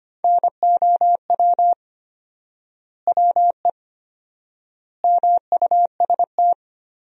Morse code, heard as a keyed tone, about 25 wpm, 700 Hz. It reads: NOW WE MUST
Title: 4x Word-Spacing - Sentences from Top 500 Words - 25wpm